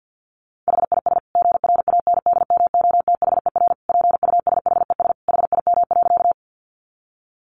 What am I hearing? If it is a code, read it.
Morse code, heard as a keyed tone, about 50 wpm, 700 Hz. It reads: HIS GRANDMOTHER PUSHES HIM.
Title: Additional Easy Phrases 50wpm